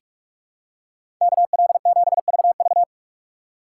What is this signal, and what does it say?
Morse code, heard as a keyed tone, about 43 wpm, 700 Hz. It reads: KL7VV
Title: USA Call Signs - 5 Characters - 2x Character Spacing - 45wpm